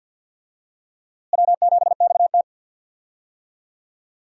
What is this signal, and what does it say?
Morse code, heard as a keyed tone, about 48 wpm, 700 Hz. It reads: W7XT
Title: USA Call Signs - 4 Characters - Repeated 2x - 50wpm